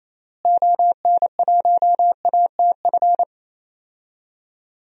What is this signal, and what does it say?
Morse code, heard as a keyed tone, about 28 wpm, 700 Hz. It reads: ON1ATF
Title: DXCC-100 Call Signs - 6 Characters - Repeated 3x - 28wpm